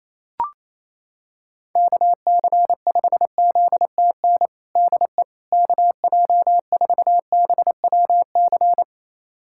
Morse code, heard as a keyed tone, about 28 wpm, 700 Hz. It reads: KC5ZTN DE KJ4BWC
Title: USA Callsign DE Callsign 28wpm